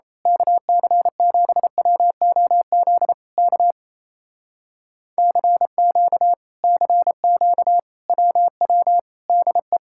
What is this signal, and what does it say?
Morse code, heard as a keyed tone, about 26 wpm, 700 Hz. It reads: KC7WOZ K CQ CQ WW DE
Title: Speed-Racing Common QSO Phrases - 28wpm